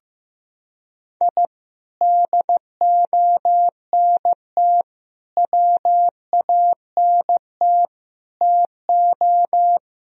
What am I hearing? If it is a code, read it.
Morse code, heard as a keyed tone, about 15 wpm, 700 Hz. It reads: I DONT WANT TO
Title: Sentences from Top 1000 Words - Review Part 1 - 15wpm